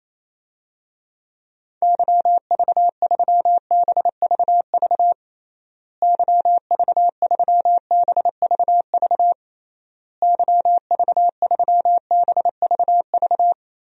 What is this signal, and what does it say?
Morse code, heard as a keyed tone, about 27 wpm, 700 Hz. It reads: YV3BVV YV3BVV YV3BVV
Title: DXCC-100 Call Signs - 6 Characters - Repeated 3x - 28wpm